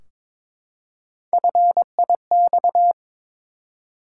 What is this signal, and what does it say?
Morse code, heard as a keyed tone, about 22 wpm, 700 Hz. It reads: FIX
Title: Three Letter Words - 22wpm